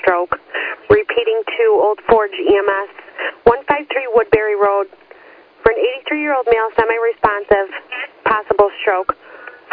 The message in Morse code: none

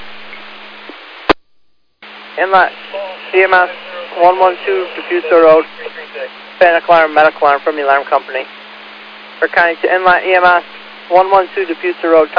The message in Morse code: none

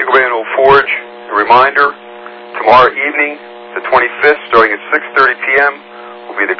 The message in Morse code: none

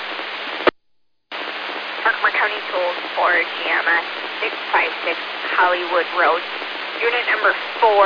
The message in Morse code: none

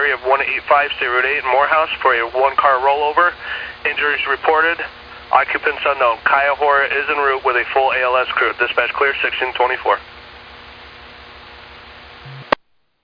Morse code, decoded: none

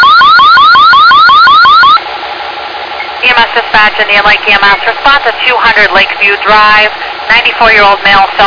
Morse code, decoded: none

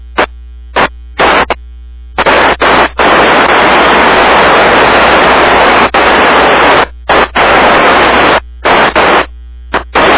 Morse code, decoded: none

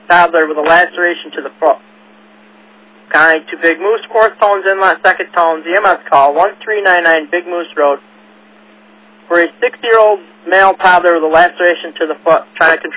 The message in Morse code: none

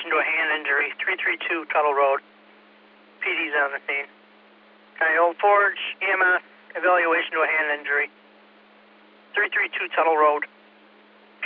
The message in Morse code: none

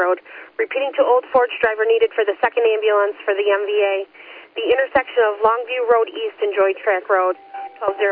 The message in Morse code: none

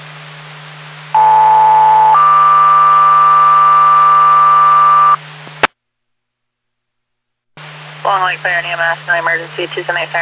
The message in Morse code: none

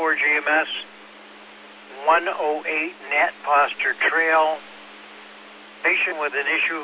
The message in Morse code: none

